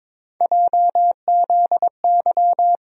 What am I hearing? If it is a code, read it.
Morse code, heard as a keyed tone, about 22 wpm, 700 Hz. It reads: JZY
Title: USA Call Sign Postfixes - 3 Letters 22wpm